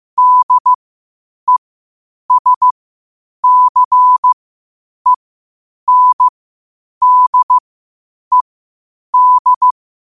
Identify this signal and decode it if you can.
Morse code, heard as a keyed tone, about 15 wpm, spaced as for 8 wpm, 1000 Hz. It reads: DESCENDED
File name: qotdfiles / 8wpm